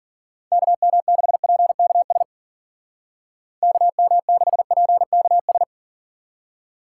Morse code, heard as a keyed tone, about 53 wpm, 700 Hz. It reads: KM6PKS KM6PKS